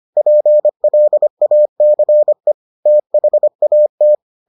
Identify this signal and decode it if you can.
Morse code, heard as a keyed tone, about 25 wpm, 600 Hz. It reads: PLACE THAT